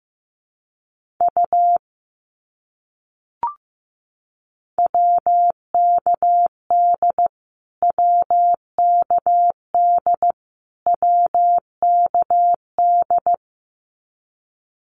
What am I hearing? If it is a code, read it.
Morse code, heard as a keyed tone, about 15 wpm, 700 Hz. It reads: U WKD WKD WKD
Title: Abbreviations Repeated 3x - 15wpm